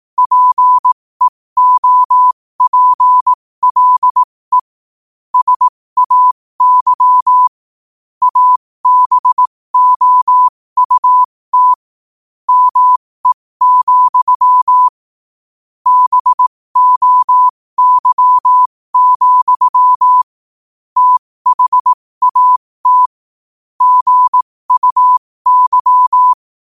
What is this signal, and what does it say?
Morse code, heard as a keyed tone, about 18 wpm, spaced as for 15 wpm, 1000 Hz. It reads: PEOPLE SAY ABOUT ME, BOY, THAT GUY
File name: qotdfiles / 15wpm